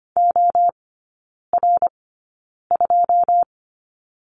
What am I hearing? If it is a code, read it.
Morse code, heard as a keyed tone, about 25 wpm, spaced as for 9 wpm, 700 Hz. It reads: OR2
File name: Lesson 17e - Letter u - Three Characters 25wpm